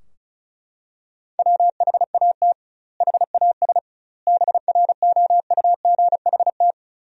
Morse code, heard as a keyed tone, about 34 wpm, 700 Hz. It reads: WHAT HAS BROUGHT
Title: 2x Word-Spacing - Sentences from Top 500 Words - 35wpm